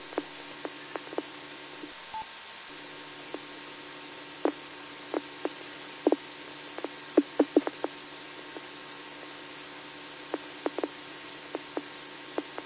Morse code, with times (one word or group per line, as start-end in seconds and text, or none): none